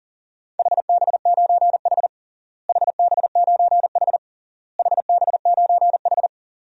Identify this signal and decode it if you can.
Morse code, heard as a keyed tone, about 37 wpm, 700 Hz. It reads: HB9H HB9H HB9H
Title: DXCC-100 Call Signs - 4 Characters - Repeated 3x - 40wpm